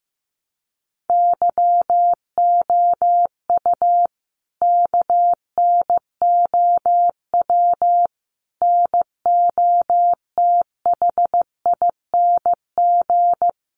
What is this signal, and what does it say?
Morse code, heard as a keyed tone, about 15 wpm, 700 Hz. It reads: YOU KNOW NOTHING
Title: Sentences from Top 500 Words 15wpm